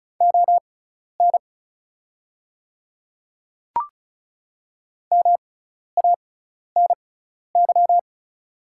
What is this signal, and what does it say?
Morse code, heard as a keyed tone, about 36 wpm, spaced as for 12 wpm, 690 Hz. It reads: ON MANY
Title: 6x Character Spacing - Top 300 Words - 35wpm